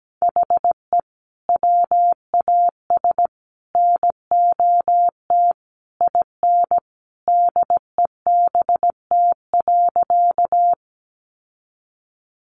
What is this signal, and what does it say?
Morse code, heard as a keyed tone, about 17 wpm, 700 Hz. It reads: HE WAS NOT IN DEBT.
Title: Sentences from Top 1300 Words 17wpm